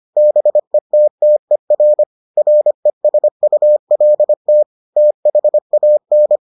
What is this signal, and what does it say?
Morse code, heard as a keyed tone, about 25 wpm, 600 Hz. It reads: BETTER RESULT THAN